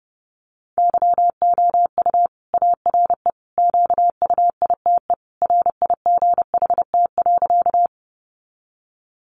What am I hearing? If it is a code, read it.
Morse code, heard as a keyed tone, about 30 wpm, 700 Hz. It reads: YOU ARE QUITE RIGHT.
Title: Sentences from Top 700 Words 30wpm